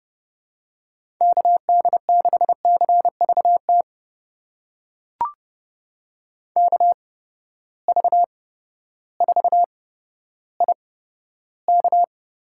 Morse code, heard as a keyed tone, about 31 wpm, spaced as for 14 wpm, 700 Hz. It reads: KD6CVT KV4IK
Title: USA Call Signs - 8x Character Spacing - 30wpm